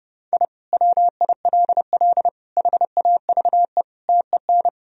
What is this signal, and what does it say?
Morse code, heard as a keyed tone, about 30 wpm, 700 Hz. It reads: I WILL HAVE TEN
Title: Sentences from Top 1000 Words - Review Part 2 - 30wpm